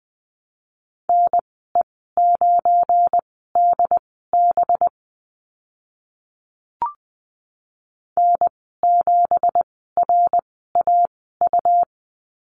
Morse code, heard as a keyed tone, about 20 wpm, spaced as for 14 wpm, 700 Hz. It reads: NE9DB N7RAU
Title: 193 Lesson 31g - Letter v - Call Signs 20wpm